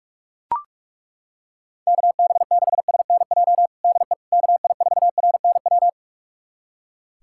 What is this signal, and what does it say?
Morse code, heard as a keyed tone, about 41 wpm, 700 Hz. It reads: KB6SNJ DE KI4RNW
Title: USA Callsign DE Callsign - 6 Characters - 45wpm